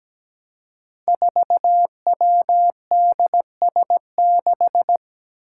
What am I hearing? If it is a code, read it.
Morse code, heard as a keyed tone, about 17 wpm, 700 Hz. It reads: 4WDS6
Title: Five Challenging Letter-Number Combinations 17wpm